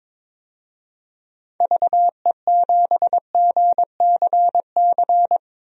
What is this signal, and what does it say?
Morse code, heard as a keyed tone, about 21 wpm, 700 Hz. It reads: VE7GCC
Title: DXCC-100 Call Signs - 6 Characters - 22wpm